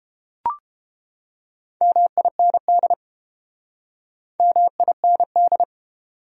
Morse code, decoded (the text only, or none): MIND MIND